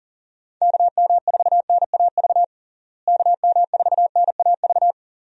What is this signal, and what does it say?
Morse code, heard as a keyed tone, about 40 wpm, 700 Hz. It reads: KM4NAV KM4NAV
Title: USA Call Signs - 6 Characters - Repeated 3x - 40wpm